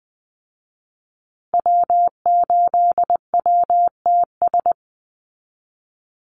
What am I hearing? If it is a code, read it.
Morse code, heard as a keyed tone, about 20 wpm, 700 Hz. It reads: W8WTS
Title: Call Signs Repeated 2x - 20wpm